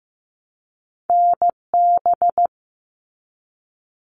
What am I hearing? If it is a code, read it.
Morse code, heard as a keyed tone, about 15 wpm, 700 Hz. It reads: NB
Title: Canadian Provinces and Territories 15wpm